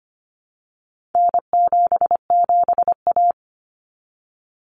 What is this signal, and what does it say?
Morse code, heard as a keyed tone, about 25 wpm, 700 Hz. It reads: N77A
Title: Common Cut Numbers - 4-Digit 25wpm